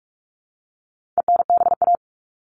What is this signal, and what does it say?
Morse code, heard as a keyed tone, about 44 wpm, 700 Hz. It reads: ENBA
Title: All Cut Numbers - 4-Digit 45wpm